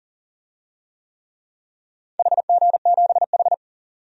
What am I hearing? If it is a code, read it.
Morse code, heard as a keyed tone, about 40 wpm, 700 Hz. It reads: HG7H
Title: DXCC-100 Call Signs - 4 Characters - 40wpm